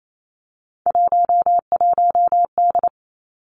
Morse code, heard as a keyed tone, about 28 wpm, 700 Hz. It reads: 11D
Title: Three Challenging Letter-Number Combinations 28wpm